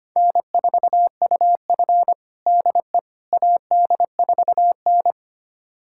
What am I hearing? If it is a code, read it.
Morse code, heard as a keyed tone, about 25 wpm, 700 Hz. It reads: N4UF DE AD4N
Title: USA Callsign DE Callsign - 4 Characters - 25wpm